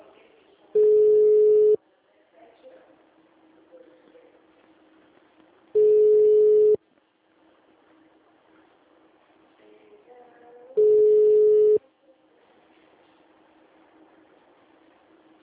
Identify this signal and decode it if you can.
Ringback tone (Europe, 425 Hz).